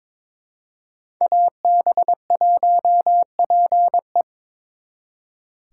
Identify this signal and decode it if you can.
Morse code, heard as a keyed tone, about 26 wpm, 700 Hz. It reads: AB1PE
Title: Speed Racing - USA 2x2 Call Signs 22wpm